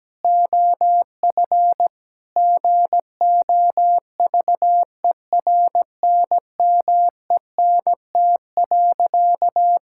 Morse code, heard as a keyed tone, about 17 wpm, 700 Hz. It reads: OF GOVERNMENT.
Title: Sentences from Top 1000 Words - Review Part 5 - 17wpm